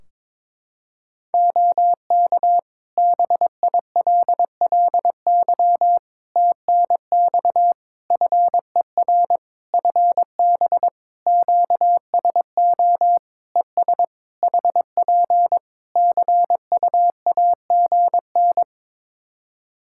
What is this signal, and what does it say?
Morse code, heard as a keyed tone, about 22 wpm, 700 Hz. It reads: OK BILLY TNX FER FB QSO ES HP CUAGN
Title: Practice QSO - Exchange 4 - Ending - 22wpm